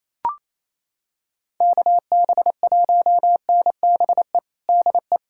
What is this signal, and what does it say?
Morse code, heard as a keyed tone, about 28 wpm, 700 Hz. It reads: KB1NBE DE
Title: USA Callsign DE Callsign - 6 Characters - 28wpm